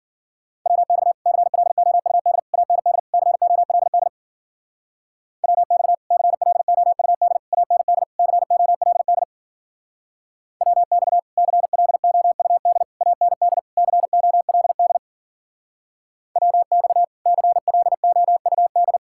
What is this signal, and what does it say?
Morse code, heard as a keyed tone, about 50 wpm, 700 Hz. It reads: WX CLOUD AND COLD WX CLOUD AND COLD WX CLOUD AND COLD WX CLOUD
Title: Speed-Racing Common QSO Phrases - 40wpm